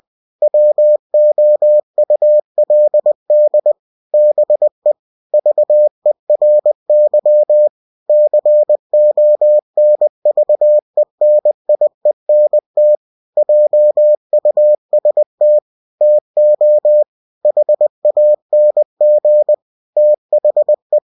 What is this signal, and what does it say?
Morse code, heard as a keyed tone, about 20 wpm, 600 Hz. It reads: WOULD BE VERY CONVENIENT JUST TO HANG THE
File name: cw-books / winnie-the-pooh / winnie-20wpm-page-0019